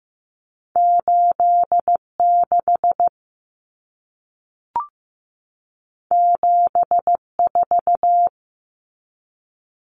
Morse code, heard as a keyed tone, about 15 wpm, 700 Hz. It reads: 86 74
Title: Numbers - 2-Digit 15wpm